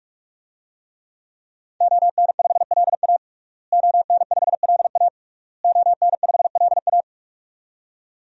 Morse code, heard as a keyed tone, about 45 wpm, 700 Hz. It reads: ON5LA ON5LA ON5LA
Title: DXCC-100 Call Signs - 5 Characters - Repeated 3x - 45wpm